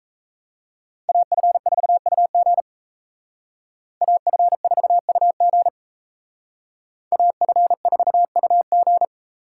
Morse code, heard as a keyed tone, about 32 wpm, 700 Hz. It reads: AF4UG AF4UG AF4UG
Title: Speed Racing - USA 2x2 Call Signs 28wpm